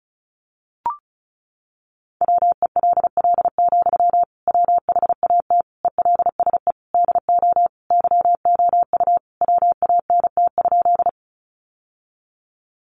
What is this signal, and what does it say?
Morse code, heard as a keyed tone, about 34 wpm, 700 Hz. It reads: WELL, WHAT ELSE DO YOU WANT?
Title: Sentences from Top 800 Words 35wpm